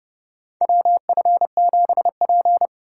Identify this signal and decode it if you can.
Morse code, heard as a keyed tone, about 30 wpm, 700 Hz. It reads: WF7P
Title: USA Call Signs - 4 Characters - 2x Character Spacing - 30wpm